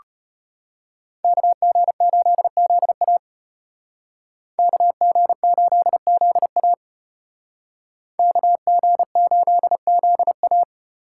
Morse code, heard as a keyed tone, about 32 wpm, 700 Hz. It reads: KG8ZA KG8ZA KG8ZA